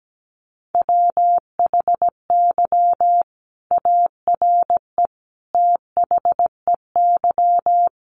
Morse code, heard as a keyed tone, about 17 wpm, 700 Hz. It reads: WHY ARE THEY
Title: Sentences from Top 200 Words 17wpm